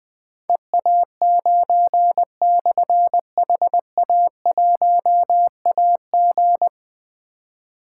Morse code, heard as a keyed tone, about 20 wpm, 700 Hz. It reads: EA9/HA1AG
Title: Difficult Call Signs 20wpm